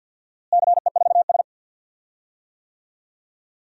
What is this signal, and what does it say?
Morse code, heard as a keyed tone, about 52 wpm, 700 Hz. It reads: CE4S
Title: DXCC-100 Call Signs - 4 Characters - Repeated 2x - 50wpm